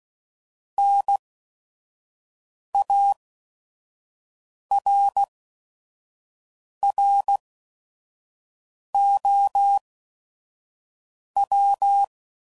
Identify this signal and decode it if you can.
Morse code, heard as a keyed tone, about 16 wpm, spaced as for 5 wpm, 800 Hz. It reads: NARROW
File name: Prefixes / By First Letter / Ns